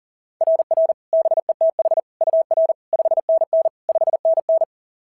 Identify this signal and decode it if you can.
Morse code, heard as a keyed tone, about 40 wpm, 690 Hz. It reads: RR BETH UR 5NN 5NN
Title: SOTA QSO Practice 40wpm